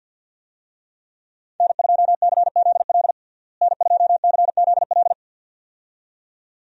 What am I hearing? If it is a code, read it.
Morse code, heard as a keyed tone, about 52 wpm, 700 Hz. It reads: N2CZL N2CZL
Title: USA Call Signs - 5 Characters - Repeated 2x - 50wpm